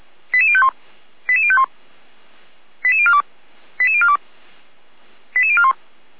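Five-tone selective call (ZVEI), heard as quick runs of five tones, tones of about 70 ms. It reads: 88051 88051 88042 88042 88041
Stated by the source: ZVEI five-tone selective call 88090 received on SDR.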